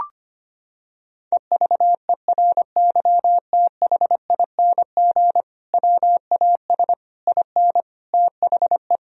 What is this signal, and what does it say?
Morse code, heard as a keyed tone, about 25 wpm, 700 Hz. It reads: EVERYTHING WAS IN THE
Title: Sentences from Top 1000 Words - Review Part 2 - 25wpm